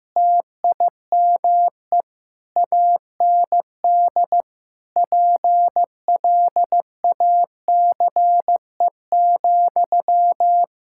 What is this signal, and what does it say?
Morse code, heard as a keyed tone, about 16 wpm, 700 Hz. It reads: TIME AND PLACE,
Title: Sentences from Top 1000 Words - Review Part 5 - 15wpm